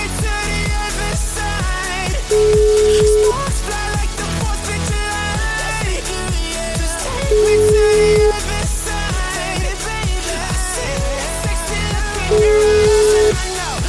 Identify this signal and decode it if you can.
Ringback tone (Europe, 425 Hz), with noise.